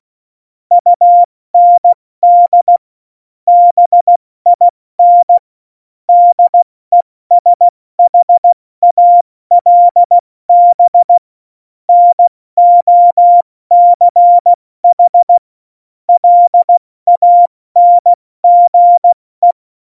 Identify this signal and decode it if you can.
Morse code, heard as a keyed tone, about 16 wpm, 700 Hz. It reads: UND BIN DESHALB NOCH LANGE
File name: Morsekurs / Witze / Witz 06 March 2018, 70 F80